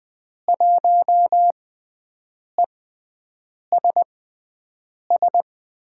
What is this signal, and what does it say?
Morse code, heard as a keyed tone, about 20 wpm, spaced as for 7 wpm, 700 Hz. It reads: 1ESS
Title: USA Call Signs - 6 Characters - 6x Character Spacing - 20wpm